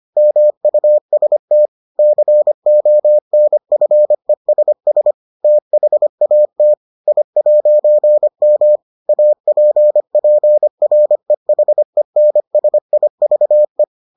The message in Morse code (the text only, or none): MUST CONFESS THAT I'M APPREHENSIVE